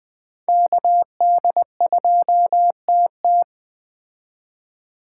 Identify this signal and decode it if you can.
Morse code, heard as a keyed tone, about 21 wpm, 700 Hz. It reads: KD2TT
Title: Speed Racing - USA Call Signs 17wpm